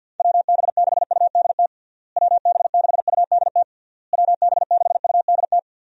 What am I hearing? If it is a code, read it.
Morse code, heard as a keyed tone, about 49 wpm, 700 Hz. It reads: WB6UDT WB6UDT WB6UDT
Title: USA Call Signs - 6 Characters - Repeated 3x - 50wpm